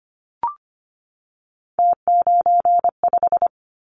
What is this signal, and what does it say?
Morse code, heard as a keyed tone, about 25 wpm, 700 Hz. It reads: T95
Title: Callsign Prefixes for Top 100 DXCC - 25wpm